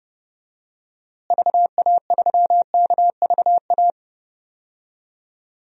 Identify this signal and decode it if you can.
Morse code, heard as a keyed tone, about 32 wpm, 700 Hz. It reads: VA3KVA
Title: DXCC-100 Call Signs - 6 Characters - Repeated 2x - 30wpm